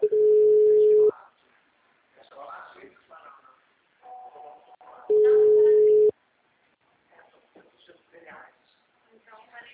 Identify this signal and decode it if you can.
Ringback tone (Europe, 425 Hz), with speech.